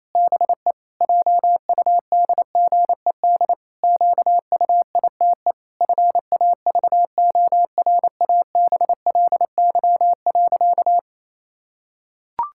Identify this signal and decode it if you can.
Morse code, heard as a keyed tone, about 28 wpm, 700 Hz. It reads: BE JUDGED QUITE FAVORABLY.
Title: Pangram Challenge v2 - 28wpm